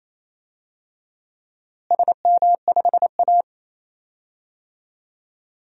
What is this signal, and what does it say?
Morse code, heard as a keyed tone, about 28 wpm, 700 Hz. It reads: SM5A
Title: DXCC-100 Call Signs - 4 Characters - Repeated 3x - 28wpm